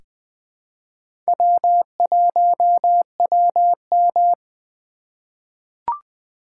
Morse code, heard as a keyed tone, about 20 wpm, 700 Hz. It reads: W1WM
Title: USA Call Signs - 4 Characters - 4x Character Spacing - 20wpm